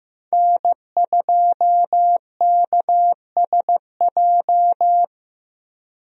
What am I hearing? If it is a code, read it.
Morse code, heard as a keyed tone, about 15 wpm, 700 Hz. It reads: N2KSJ
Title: USA Call Signs - 5 Characters - Repeated 3x - 15wpm